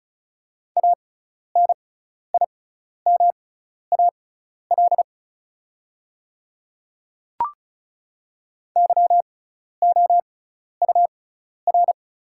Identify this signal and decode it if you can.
Morse code, heard as a keyed tone, about 36 wpm, spaced as for 12 wpm, 700 Hz. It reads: ANIMAL YOUR
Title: 6x Character Spacing - Top 200 Words - 35wpm